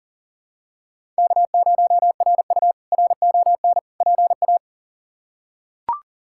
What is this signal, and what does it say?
Morse code, heard as a keyed tone, about 37 wpm, 700 Hz. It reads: K0RU RON PA
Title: K1USN SST QSO Practice - 40wpm